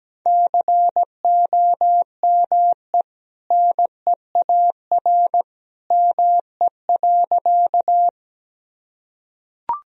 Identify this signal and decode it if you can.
Morse code, heard as a keyed tone, about 17 wpm, 700 Hz. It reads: COME NEAR ME.
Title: Sentences from Top 1000 Words - Review Part 1 - 17wpm